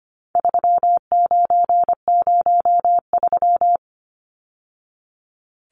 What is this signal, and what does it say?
Morse code, heard as a keyed tone, about 25 wpm, 700 Hz. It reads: 3903
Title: Numbers - 4-Digit 25wpm